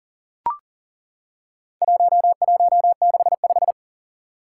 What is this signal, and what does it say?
Morse code, heard as a keyed tone, about 37 wpm, 700 Hz. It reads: 1165
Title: Four Challenging Letter-Number Combinations 40wpm